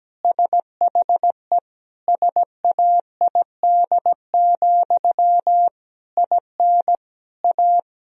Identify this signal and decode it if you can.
Morse code, heard as a keyed tone, about 17 wpm, 700 Hz. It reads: SHE SAID, IN A
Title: Sentences from Top 1000 Words - Review Part 1 - 17wpm